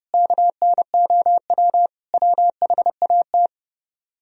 Morse code, heard as a keyed tone, about 38 wpm, 700 Hz. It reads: KNOW WHAT